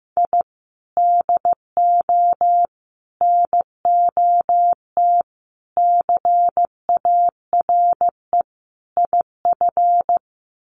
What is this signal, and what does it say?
Morse code, heard as a keyed tone, about 15 wpm, 700 Hz. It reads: I DO NOT CARE IF
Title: Sets of 5 Words 15wpm